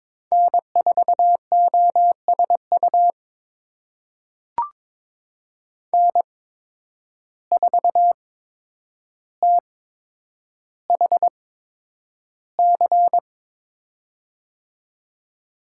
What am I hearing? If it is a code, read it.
Morse code, heard as a keyed tone, about 22 wpm, spaced as for 9 wpm, 700 Hz. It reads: N4OSU N4THC
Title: USA Call Signs - 5 Characters - 8x Character Spacing - 22wpm